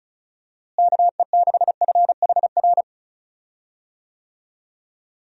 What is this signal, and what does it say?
Morse code, heard as a keyed tone, about 34 wpm, 700 Hz. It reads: KE6FHR
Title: USA Call Signs - 6 Characters - Repeated 2x - 35wpm